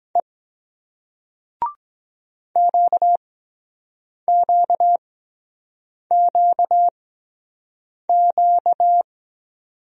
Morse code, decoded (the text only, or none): E QQQQ